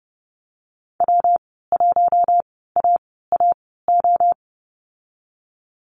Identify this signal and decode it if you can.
Morse code, heard as a keyed tone, about 30 wpm, spaced as for 17 wpm, 700 Hz. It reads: W1AAO
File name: Lesson 19g - Letter w - Call Signs 30wpm